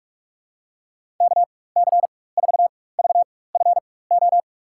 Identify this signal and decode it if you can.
Morse code, heard as a keyed tone, about 46 wpm, spaced as for 21 wpm, 700 Hz. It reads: KC4VFO
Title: USA Call Signs - 6 Characters - 4x Character Spacing - 45wpm